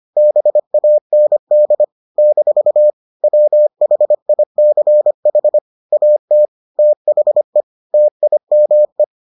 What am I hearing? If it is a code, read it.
Morse code, heard as a keyed tone, about 25 wpm, 600 Hz. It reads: BAND - WHICH AT THE TIME